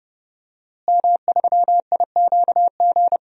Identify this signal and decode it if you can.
Morse code, heard as a keyed tone, about 30 wpm, 700 Hz. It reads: M3IQG
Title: DXCC-100 Call Signs - 5 Characters - Repeated 2x - 30wpm